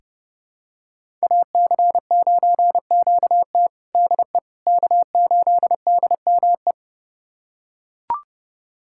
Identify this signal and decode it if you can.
Morse code, heard as a keyed tone, about 31 wpm, 700 Hz. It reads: AC9QT DE K8DME
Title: USA Callsign DE Callsign - 5 Characters - 30wpm